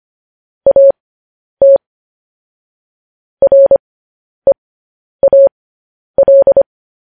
Morse code, heard as a keyed tone, about 25 wpm, spaced as for 10 wpm, 550 Hz. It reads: AT REAL